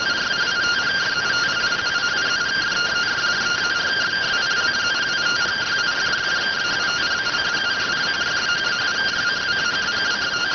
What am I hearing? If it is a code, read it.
Two-tone teleprinter signal, about 170 Hz shift.